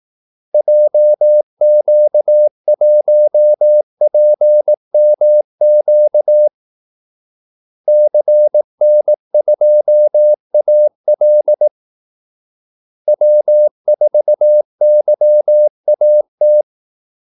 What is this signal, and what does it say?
Morse code, heard as a keyed tone, about 18 wpm, 600 Hz. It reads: JQ1PMQ CN2AL W4YAT
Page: Kaldesignaler 18 wpm